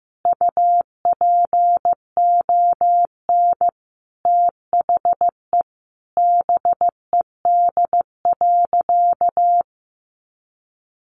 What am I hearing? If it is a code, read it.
Morse code, heard as a keyed tone, about 15 wpm, 700 Hz. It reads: UPON THE BED.
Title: Sentences from Top 1100 Words 15wpm